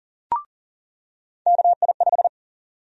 Morse code, heard as a keyed tone, about 40 wpm, 700 Hz. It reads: KI5
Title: USA Call Sign Prefixes - 40wpm